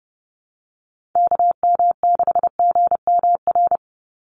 Morse code, heard as a keyed tone, about 30 wpm, 700 Hz. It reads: KM6GMR
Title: USA Call Signs 30wpm